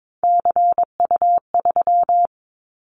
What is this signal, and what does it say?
Morse code, heard as a keyed tone, about 22 wpm, 700 Hz. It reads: CU3
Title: Callsign Prefixes for Top 100 DXCC - 22wpm